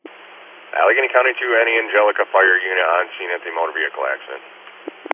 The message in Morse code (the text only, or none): none